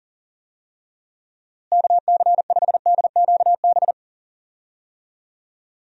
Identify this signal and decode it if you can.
Morse code, heard as a keyed tone, about 40 wpm, 700 Hz. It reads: KC5DQB
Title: USA Call Signs - 6 Characters - Repeated 3x - 40wpm